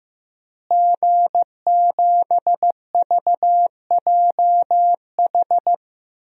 Morse code, heard as a keyed tone, about 15 wpm, 700 Hz. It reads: G7VJH
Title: DXCC-100 Call Signs - 5 Characters - Repeated 3x - 15wpm